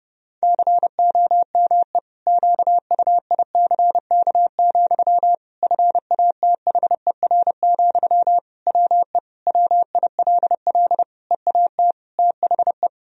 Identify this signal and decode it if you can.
Morse code, heard as a keyed tone, about 30 wpm, 700 Hz. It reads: COME QUICK, FATHER, WE WILL EAT THE
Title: Sentences from Top 1000 Words - Review Part 3 - 30wpm